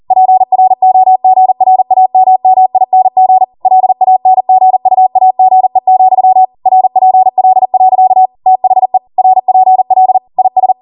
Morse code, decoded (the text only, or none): PROGRAMMING LANGUAGE, RPL. THE RPL IS